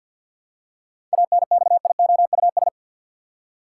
Morse code, heard as a keyed tone, about 47 wpm, 700 Hz. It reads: ANXIOUS
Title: Seven Letter Words - 50wpm